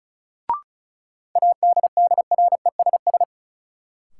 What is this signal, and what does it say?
Morse code, heard as a keyed tone, about 34 wpm, 700 Hz. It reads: ADDRESS
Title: Sentences from Top 1000 Words - New Word Drill - 35wpm